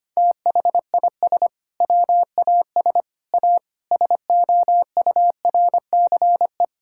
Morse code, heard as a keyed tone, about 25 wpm, 700 Hz. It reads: THIS WAS A SOURCE
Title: Sentences from Top 1000 Words - Review Part 5 - 25wpm